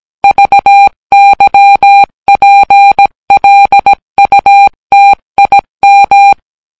Morse code, heard as a keyed tone, about 17 wpm, 800 Hz. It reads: VYPLUTIM